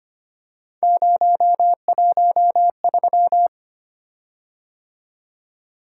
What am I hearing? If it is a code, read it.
Morse code, heard as a keyed tone, about 25 wpm, 700 Hz. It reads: 013
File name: Three Letter-Number Combinations - 25wpm